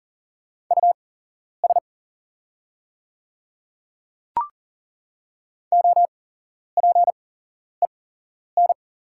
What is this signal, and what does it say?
Morse code, heard as a keyed tone, about 38 wpm, spaced as for 11 wpm, 700 Hz. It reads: US OPEN
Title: 8x Character Spacing - Top 300 Words - 40wpm